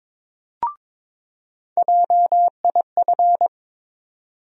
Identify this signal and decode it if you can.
Morse code, heard as a keyed tone, about 21 wpm, 700 Hz. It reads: JIF
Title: USA Call Sign Postfixes - 3 Letters 22wpm